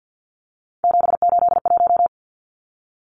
Morse code, heard as a keyed tone, about 50 wpm, 700 Hz. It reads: 781
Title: Numbers - 3-Digit 50wpm